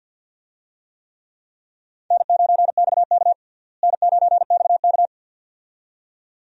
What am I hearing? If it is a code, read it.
Morse code, heard as a keyed tone, about 47 wpm, 690 Hz. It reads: N9XK N9XK
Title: USA Call Signs - 4 Characters - Repeated 2x - 50wpm